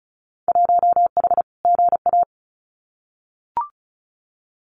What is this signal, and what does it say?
Morse code, heard as a keyed tone, about 34 wpm, 700 Hz. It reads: 1H GA
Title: Winter Field Day Exchanges 35wpm